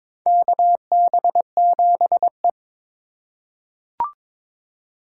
Morse code, heard as a keyed tone, about 22 wpm, 700 Hz. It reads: KB7E
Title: USA Call Signs - 4 Characters - 4x Character Spacing - 22wpm